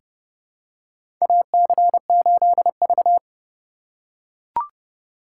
Morse code, heard as a keyed tone, about 30 wpm, 700 Hz. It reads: AC8V
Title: USA Call Signs - 3x Character Spacing - 30wpm